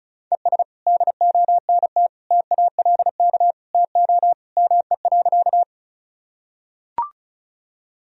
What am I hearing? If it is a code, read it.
Morse code, heard as a keyed tone, about 34 wpm, 700 Hz. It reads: ES DONT TALK TO ME.
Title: Sentences from Top 400 Words and Abbreviations 35wpm